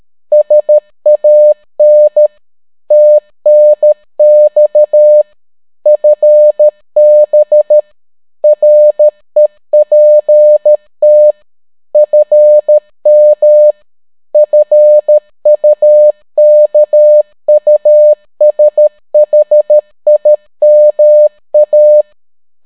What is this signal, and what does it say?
Morse code, heard as a keyed tone, about 13 wpm, 600 Hz. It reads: SAN TNX FB REPT FM FUKUSHIMA